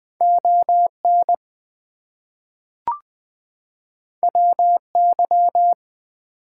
Morse code, heard as a keyed tone, about 20 wpm, 700 Hz. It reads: ON WY
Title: US States and Canadian Provinces 20wpm